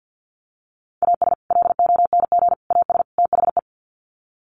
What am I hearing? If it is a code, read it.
Morse code, heard as a keyed tone, about 50 wpm, 700 Hz. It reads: AS LONG AS THE
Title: Sets of 4 Words 50wpm